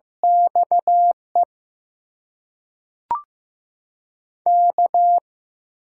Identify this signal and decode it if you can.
Morse code, heard as a keyed tone, about 15 wpm, 700 Hz. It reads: XE K
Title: USA Call Signs - 6 Characters - 6x Character Spacing - 15wpm